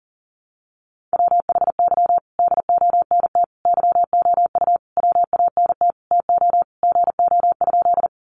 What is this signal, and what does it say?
Morse code, heard as a keyed tone, about 40 wpm, 700 Hz. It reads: WHY DONT YOU WANT TO GO?
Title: Sentences from Top 300 Words 40wpm